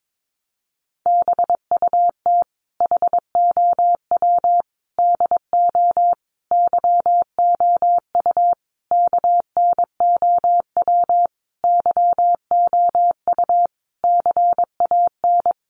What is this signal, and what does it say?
Morse code, heard as a keyed tone, about 22 wpm, 700 Hz. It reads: BUT HOW DO YOU KNOW YOU CAN
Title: Sentences from Top 900 Words 22wpm